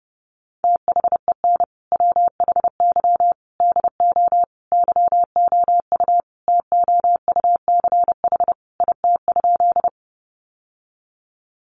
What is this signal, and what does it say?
Morse code, heard as a keyed tone, about 28 wpm, 700 Hz. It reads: THEN WHY DO YOU TOUCH IT?
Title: Sentences from Top 1200 Words 30wpm